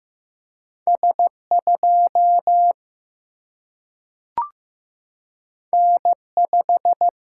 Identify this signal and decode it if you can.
Morse code, heard as a keyed tone, about 15 wpm, 700 Hz. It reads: S 2 N 5
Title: Single-Letter Single-Number 15wpm